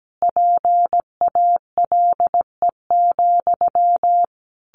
Morse code, heard as a keyed tone, about 17 wpm, 700 Hz. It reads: PALE,
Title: A Sound Of Thunder 17wpm